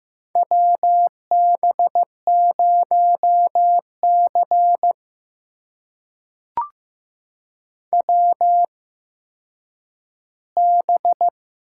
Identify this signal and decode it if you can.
Morse code, heard as a keyed tone, about 15 wpm, spaced as for 9 wpm, 700 Hz. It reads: WB0C WB
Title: USA Call Signs - 4 Characters - 8x Character Spacing - 15wpm